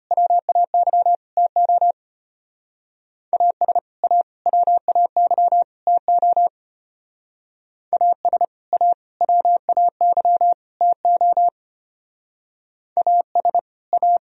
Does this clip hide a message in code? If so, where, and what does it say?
Yes: Morse code from 0.1 to 14.2 s, heard as a keyed tone, about 31 wpm, 700 Hz. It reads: WAY TO AS A WAY TO AS A WAY TO AS A